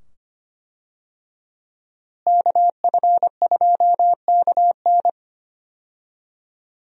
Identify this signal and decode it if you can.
Morse code, heard as a keyed tone, about 25 wpm, 700 Hz. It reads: KF2KN
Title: USA Call Signs - 5 Characters - Repeated 2x - 25wpm